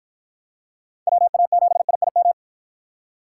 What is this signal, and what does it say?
Morse code, heard as a keyed tone, about 52 wpm, 700 Hz. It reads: WA7IIM